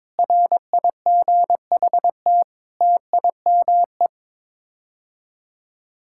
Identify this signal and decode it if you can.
Morse code, heard as a keyed tone, about 22 wpm, 700 Hz. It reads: RIGHT TIME
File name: practice / 22wpm / Sets of 7 Words - 22wpm